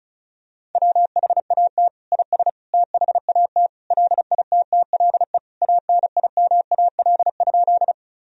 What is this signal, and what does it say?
Morse code, heard as a keyed tone, about 36 wpm, 700 Hz. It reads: WHAT IS THAT LITTLE ANIMAL?
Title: Sentences from Top 1000 Words - Review Part 1 - 35wpm